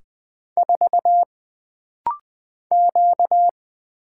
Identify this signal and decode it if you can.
Morse code, heard as a keyed tone, about 20 wpm, 700 Hz. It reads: 4 Q
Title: Single Letter-Number Alt - Rapid-Fire 20wpm